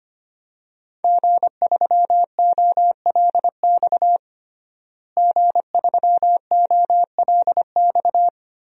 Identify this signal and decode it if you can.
Morse code, heard as a keyed tone, about 25 wpm, 700 Hz. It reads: G3OLX G3OLX
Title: DXCC-100 Call Signs - 5 Characters - Repeated 3x - 25wpm